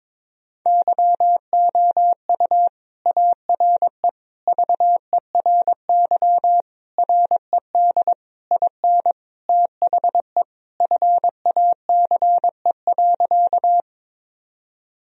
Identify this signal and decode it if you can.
Morse code, heard as a keyed tone, about 22 wpm, 700 Hz. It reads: YOU ARE VERY RED IN THE FACE.
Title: Sentences from Top 1000 Words - Review Part 2 - 22wpm